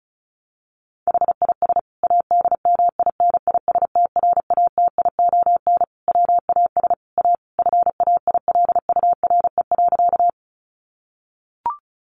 Morse code, heard as a keyed tone, about 34 wpm, 700 Hz. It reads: HIS ADMINISTRATION WAS A FAILURE.
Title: Sentences from Top 1300 Words 35wpm